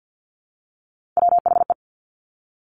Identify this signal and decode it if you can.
Morse code, heard as a keyed tone, about 50 wpm, 700 Hz. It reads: WHE
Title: Top Three Letter Combinations in Words 50wpm